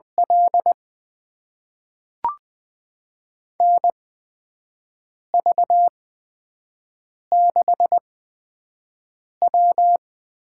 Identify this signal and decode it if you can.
Morse code, heard as a keyed tone, about 20 wpm, spaced as for 5 wpm, 700 Hz. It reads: L NV6W
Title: USA Call Signs - 8x Character Spacing - 20wpm